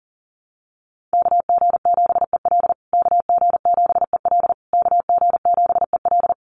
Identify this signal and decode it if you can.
Morse code, heard as a keyed tone, about 40 wpm, 700 Hz. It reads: KG7EL KG7EL KG7EL
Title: USA Call Signs Repeated 3x 40wpm